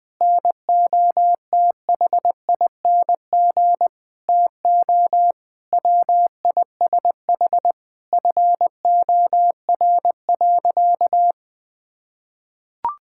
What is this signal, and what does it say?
Morse code, heard as a keyed tone, about 20 wpm, 700 Hz. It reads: NOTHING TO WISH FOR.
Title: Sentences from Top 1600 Words 20wpm